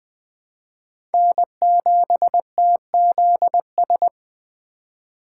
Morse code, heard as a keyed tone, about 20 wpm, 700 Hz. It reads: N7TZS
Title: USA Call Signs - 5 Characters - 3x Character Spacing - 20wpm